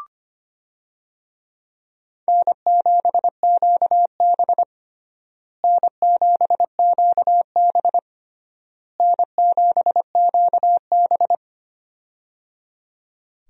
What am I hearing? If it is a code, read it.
Morse code, heard as a keyed tone, about 25 wpm, 700 Hz. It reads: N7QB N7QB N7QB
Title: USA Call Signs - 4 Characters - Repeated 3x - 25wpm